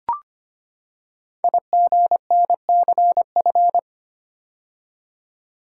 Morse code, heard as a keyed tone, about 25 wpm, 700 Hz. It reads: IGNCF
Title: Code Groups - 5 Letters - 25wpm